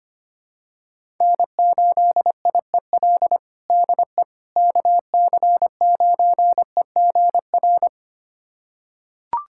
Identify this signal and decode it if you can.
Morse code, heard as a keyed tone, about 25 wpm, 700 Hz. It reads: N8IEL DE KC9EGR
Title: USA Callsign DE Callsign 25wpm